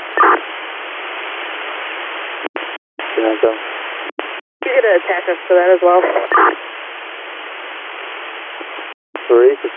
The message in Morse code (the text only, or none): none